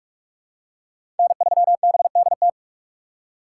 Morse code, heard as a keyed tone, about 44 wpm, 700 Hz. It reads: N3BDT
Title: USA Call Signs - 5 Characters - 4x Character Spacing - 45wpm